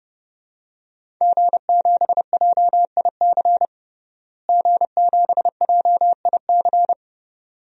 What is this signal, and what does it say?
Morse code, heard as a keyed tone, about 30 wpm, 700 Hz. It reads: G7JIC G7JIC
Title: DXCC-100 Call Signs - 5 Characters - Repeated 2x - 30wpm